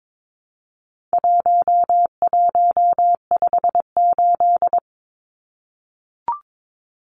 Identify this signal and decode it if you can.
Morse code, heard as a keyed tone, about 22 wpm, 700 Hz. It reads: 1158
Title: Numbers - 4-Digit 22wpm